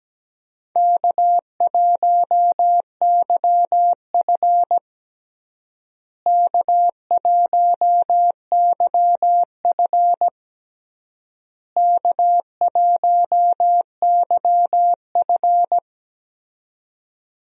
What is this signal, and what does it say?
Morse code, heard as a keyed tone, about 17 wpm, 700 Hz. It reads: K1YF K1YF K1YF
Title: USA Call Signs - 4 Characters - Repeated 3x - 17wpm